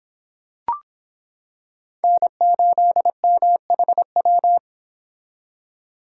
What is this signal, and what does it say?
Morse code, heard as a keyed tone, about 25 wpm, 700 Hz. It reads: N8MHW